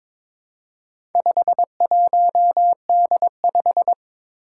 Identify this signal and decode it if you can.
Morse code, heard as a keyed tone, about 22 wpm, 700 Hz. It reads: 51D5
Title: Four Challenging Letter-Number Combinations 22wpm